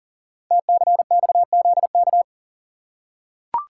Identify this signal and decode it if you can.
Morse code, heard as a keyed tone, about 38 wpm, 690 Hz. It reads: TCXZK
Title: Code Groups - 5 Letters - 40wpm